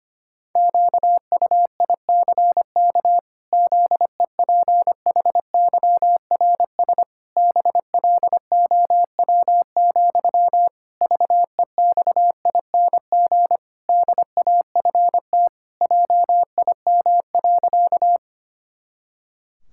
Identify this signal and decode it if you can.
Morse code, heard as a keyed tone, about 25 wpm, 700 Hz. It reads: QUICK ZEPHYRS BLOW, VEXING DAFT JIM.
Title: Pangram Challenge v2 - 25wpm